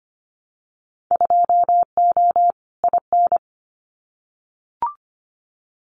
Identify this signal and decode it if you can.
Morse code, heard as a keyed tone, about 25 wpm, 700 Hz. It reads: 2O IN
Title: Winter Field Day Exchanges 25wpm